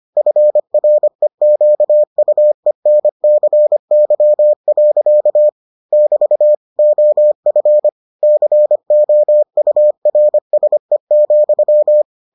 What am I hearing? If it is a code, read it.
Morse code, heard as a keyed tone, about 25 wpm, 600 Hz. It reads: FREQUENCY. <BT> OF COURSE,